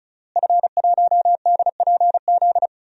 Morse code, heard as a keyed tone, about 34 wpm, 700 Hz. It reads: F1DPZ
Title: DXCC-100 Call Signs - 5 Characters - Repeated 3x - 35wpm